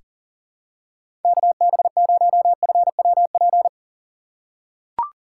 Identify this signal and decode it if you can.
Morse code, heard as a keyed tone, about 40 wpm, 700 Hz. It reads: KB0FWP
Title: USA Call Signs - 6 Characters - 3x Character Spacing - 40wpm